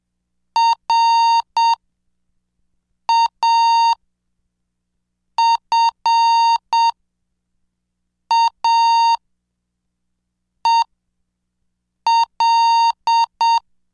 Morse code, heard as a keyed tone, about 7 wpm, spaced as for 4 wpm, 920 Hz. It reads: RAFAEL